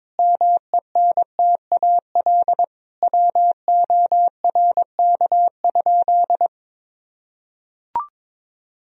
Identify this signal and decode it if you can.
Morse code, heard as a keyed tone, about 22 wpm, 700 Hz. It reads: MENTAL WORK?
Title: Sentences from Top 1000 Words - Review Part 4 - 22wpm